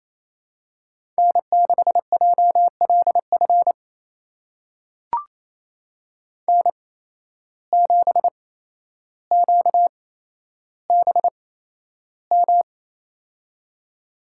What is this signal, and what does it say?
Morse code, heard as a keyed tone, about 28 wpm, spaced as for 12 wpm, 700 Hz. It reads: N6JLF N7QBM
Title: USA Call Signs - 5 Characters - 8x Character Spacing - 28wpm